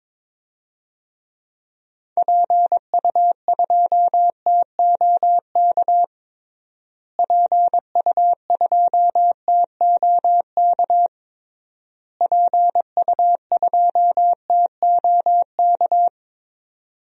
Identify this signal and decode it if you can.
Morse code, heard as a keyed tone, about 22 wpm, 700 Hz. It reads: PU2TOK PU2TOK PU2TOK
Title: DXCC-100 Call Signs - 6 Characters - Repeated 3x - 22wpm